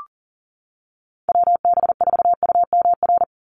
Morse code, heard as a keyed tone, about 40 wpm, 700 Hz. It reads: WB4UMR
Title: USA Call Signs 40wpm